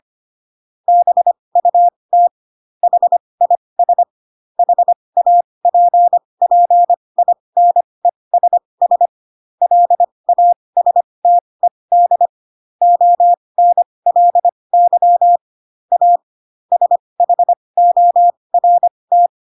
Morse code, heard as a keyed tone, about 25 wpm, spaced as for 20 wpm, 700 Hz. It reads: BUT HIS HAPPINESS LASTED ONLY A SHORT
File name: Pinocchio / Pinocchio ch4 15 25 ik1hge